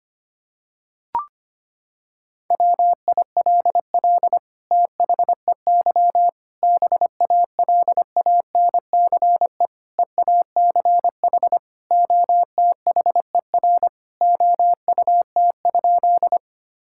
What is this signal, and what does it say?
Morse code, heard as a keyed tone, about 25 wpm, 700 Hz. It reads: WILL THEY BALANCE EACH OTHER OUT?
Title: Sentences from Top 1000 Words - Review Part 6 - 25wpm